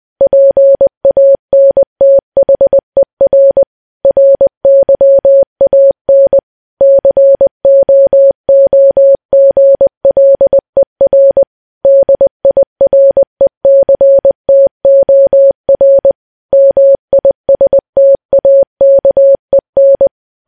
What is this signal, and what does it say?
Morse code, heard as a keyed tone, about 20 wpm, 550 Hz. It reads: PANTHER RYAN COOGLER DIRECTOR MISTAKEN
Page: news-headlines-in-morse-code-at-20-wpm